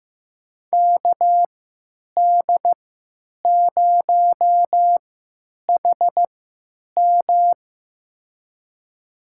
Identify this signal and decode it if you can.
Morse code, heard as a keyed tone, about 15 wpm, spaced as for 9 wpm, 700 Hz. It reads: KD0HM
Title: USA Call Signs - 3x Character Spacing - 15wpm